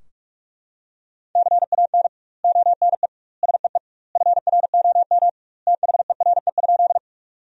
Morse code, heard as a keyed tone, about 44 wpm, 700 Hz. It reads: CAN ONE SEE FROM THERE?
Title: 2x Word-Spacing - Sentences from Top 100 Words - 45wpm